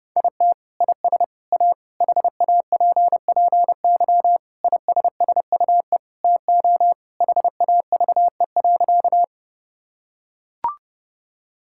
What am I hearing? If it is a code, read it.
Morse code, heard as a keyed tone, about 30 wpm, 700 Hz. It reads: IT IS A HAPPY ISSUE TO HAVE.
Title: Sentences from Top 1000 Words - Review Part 6 - 30wpm